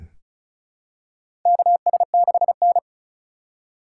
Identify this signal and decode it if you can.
Morse code, heard as a keyed tone, about 36 wpm, 700 Hz. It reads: KS6N
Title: USA Call Signs - 4 Characters - 2x Character Spacing - 35wpm